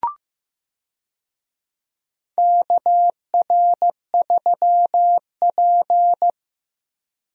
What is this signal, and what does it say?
Morse code, heard as a keyed tone, about 15 wpm, 700 Hz. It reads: KR3P
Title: USA Call Signs - 4 Characters - Repeated 2x - 15wpm